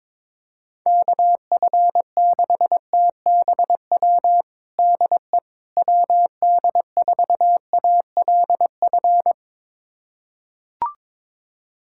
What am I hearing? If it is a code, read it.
Morse code, heard as a keyed tone, about 22 wpm, 700 Hz. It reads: KF6TBW DE WD4ALF
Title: USA Callsign DE Callsign - 6 Characters - 22wpm